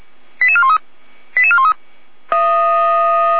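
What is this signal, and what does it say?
Five-tone selective call (ZVEI), heard as quick runs of five tones, tones of about 70 ms. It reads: 88413 88413
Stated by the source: ZVEI five-tone call 88171 received by SDR.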